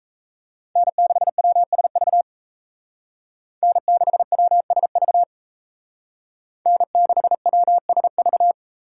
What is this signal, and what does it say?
Morse code, heard as a keyed tone, about 36 wpm, 700 Hz. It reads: N6WSV N6WSV N6WSV